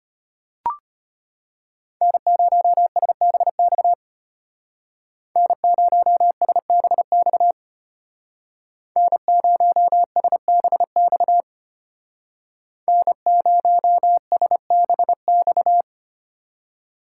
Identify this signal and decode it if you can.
Morse code, heard as a keyed tone, about 33 wpm, 700 Hz. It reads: N0SBX N0SBX N0SBX N0SBX